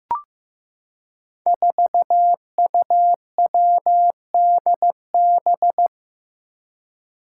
Morse code, heard as a keyed tone, about 15 wpm, 700 Hz. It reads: 4UWDB
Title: Five Challenging Letter-Number Combinations 15wpm